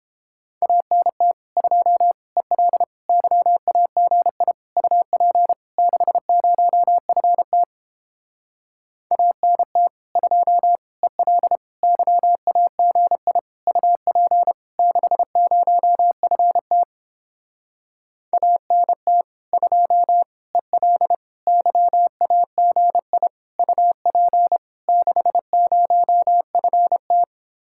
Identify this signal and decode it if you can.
Morse code, heard as a keyed tone, about 29 wpm, 700 Hz. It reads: ANT 2 EL YAGI UP 60FT ANT 2 EL YAGI UP 60FT ANT 2 EL YAGI UP 60FT